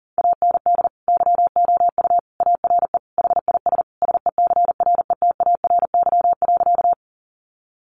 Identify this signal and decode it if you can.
Morse code, heard as a keyed tone, about 40 wpm, 700 Hz. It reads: AND YOU ARE HIS SECRETARY.
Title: Sentences from Top 1300 Words 40wpm